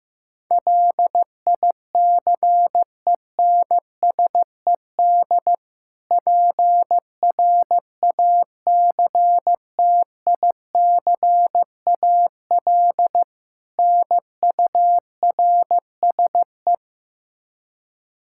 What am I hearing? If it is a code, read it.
Morse code, heard as a keyed tone, about 15 wpm, 700 Hz. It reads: LICENSED PRACTICAL NURSE
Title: Practice QSO - Exchange 3 - More Info - 15wpm